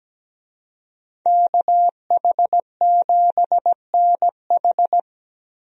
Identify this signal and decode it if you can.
Morse code, heard as a keyed tone, about 17 wpm, 700 Hz. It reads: KH7NH
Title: USA Call Signs - 5 Characters - 17wpm